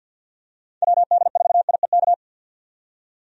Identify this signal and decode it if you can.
Morse code, heard as a keyed tone, about 51 wpm, 700 Hz. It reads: WD4IEK
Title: USA Call Signs - 6 Characters - 50wpm